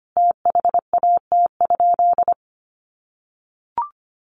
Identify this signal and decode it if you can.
Morse code, heard as a keyed tone, about 25 wpm, 700 Hz. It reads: THAT?
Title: Sentences from Top 200 Words 25wpm